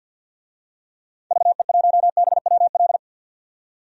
Morse code, heard as a keyed tone, about 52 wpm, 700 Hz. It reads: VE1BWL
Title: DXCC-100 Call Signs - 6 Characters - 50wpm